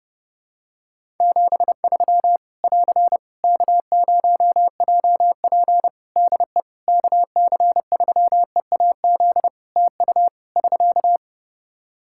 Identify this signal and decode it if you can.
Morse code, heard as a keyed tone, about 30 wpm, 700 Hz. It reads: 73 <AR> K0JP DE KC3EAZ TU <SK>
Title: Practice QSO - Exchange 4 - Ending - 30wpm